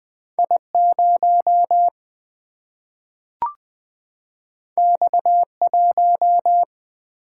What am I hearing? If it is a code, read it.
Morse code, heard as a keyed tone, about 20 wpm, 700 Hz. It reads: I 0 X 1
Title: Single-Letter Single-Number 20wpm